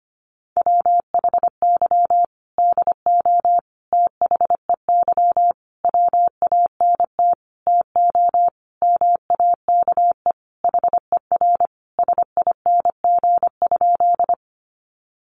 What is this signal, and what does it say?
Morse code, heard as a keyed tone, about 25 wpm, 700 Hz. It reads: WHY DO THEY WANT TO MAKE HER SING?
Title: Sentences from Top 400 Words 25wpm